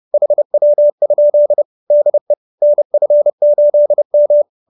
Morse code, heard as a keyed tone, about 30 wpm, 590 Hz. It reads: HW? DE NF8M